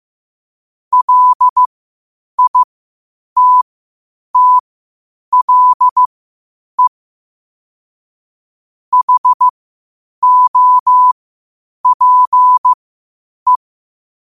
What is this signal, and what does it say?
Morse code, heard as a keyed tone, about 15 wpm, spaced as for 8 wpm, 1000 Hz. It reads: LITTLE HOPE
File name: qotdfiles / 8wpm